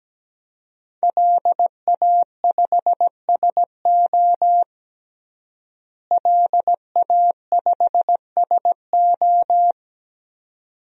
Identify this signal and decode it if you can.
Morse code, heard as a keyed tone, about 17 wpm, 700 Hz. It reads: LA5SO LA5SO
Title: DXCC-100 Call Signs - 5 Characters - Repeated 2x - 17wpm